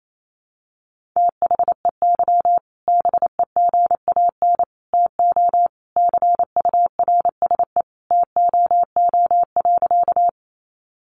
Morse code, heard as a keyed tone, about 28 wpm, 700 Hz. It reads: THEY BEGAN TO CURSE TOO.
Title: A Sound Of Thunder 28wpm